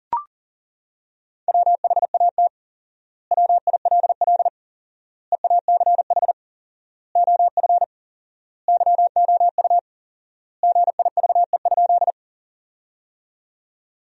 Morse code, heard as a keyed tone, about 40 wpm, 700 Hz. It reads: WHAT WILL EACH OF YOU GIVE?
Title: 4x Word-Spacing - Sentences from Top 200 Words - 40wpm